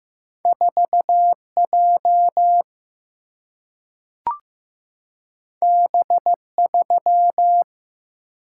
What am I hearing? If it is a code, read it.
Morse code, heard as a keyed tone, about 15 wpm, 700 Hz. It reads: 4 J B 3
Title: Single-Letter Single-Number 15wpm